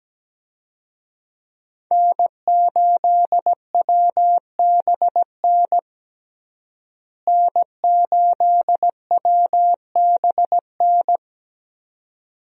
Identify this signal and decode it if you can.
Morse code, heard as a keyed tone, about 17 wpm, 700 Hz. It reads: N8WBN N8WBN
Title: USA Call Signs - 5 Characters - Repeated 2x - 17wpm